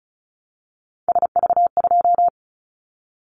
Morse code, heard as a keyed tone, about 36 wpm, 700 Hz. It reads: SV2
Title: Callsign Prefixes for Top 50 DXCC - 35wpm